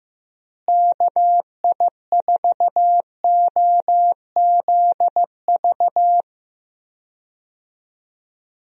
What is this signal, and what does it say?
Morse code, heard as a keyed tone, about 15 wpm, 700 Hz. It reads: KI4OZV
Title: USA Call Signs - 6 Characters - Repeated 3x - 15wpm